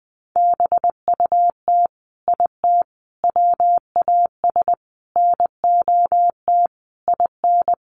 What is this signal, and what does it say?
Morse code, heard as a keyed tone, about 20 wpm, 700 Hz. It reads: BUT IT WAS NOT IN
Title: Sentences from Top 1500 Words 20wpm